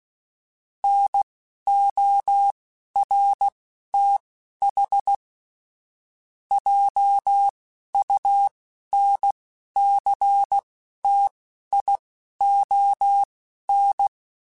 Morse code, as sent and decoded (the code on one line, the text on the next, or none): -. --- .-. - .... / .--- ..- -. -.-. - .. --- -.
NORTH JUNCTION